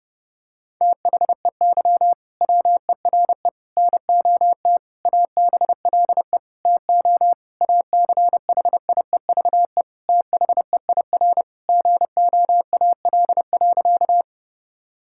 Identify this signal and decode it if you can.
Morse code, heard as a keyed tone, about 30 wpm, 700 Hz. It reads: THEY WERE NOT ABLE TO ACHIEVE THEIR GOAL.
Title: Sentences from Top 1000 Words - Review Part 6 - 30wpm